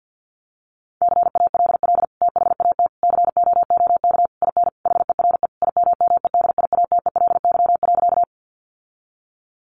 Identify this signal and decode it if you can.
Morse code, heard as a keyed tone, about 49 wpm, 700 Hz. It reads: CALL THAT COOK IN HERE IMMEDIATELY.
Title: Sentences from Top 1100 Words 50wpm